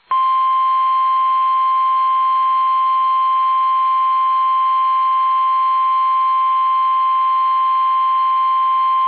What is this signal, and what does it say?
Emergency Alert System (SAME) tones (the 1050 Hz attention tone). It is heard from 0.5 to 8.5 s.
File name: eas / KUSO / KUSO